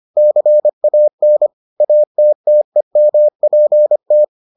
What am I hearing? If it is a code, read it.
Morse code, heard as a keyed tone, about 25 wpm, 600 Hz. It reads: CAN ATTEMPT